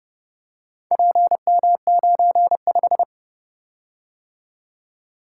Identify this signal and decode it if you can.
Morse code, heard as a keyed tone, about 31 wpm, 700 Hz. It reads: PM95